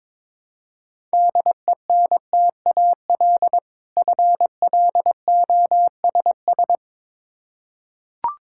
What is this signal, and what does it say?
Morse code, heard as a keyed tone, about 22 wpm, 700 Hz. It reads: DENTAL FLOSS
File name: Difficult Words with Hints - 22wpm